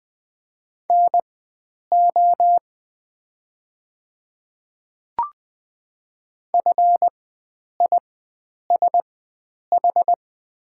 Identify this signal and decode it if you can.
Morse code, heard as a keyed tone, about 20 wpm, spaced as for 9 wpm, 700 Hz. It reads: NO FISH
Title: 4x Character Spacing - Top 300 Words - 20wpm